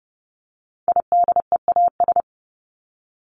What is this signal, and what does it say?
Morse code, heard as a keyed tone, about 30 wpm, 700 Hz. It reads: IDEAS
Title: ES:S Suffix - Top 100 Words with Base Words 30wpm